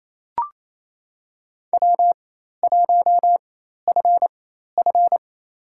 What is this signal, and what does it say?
Morse code, heard as a keyed tone, about 29 wpm, spaced as for 13 wpm, 700 Hz. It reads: W1FF
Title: USA Call Signs - 4 Characters - 4x Character Spacing - 28wpm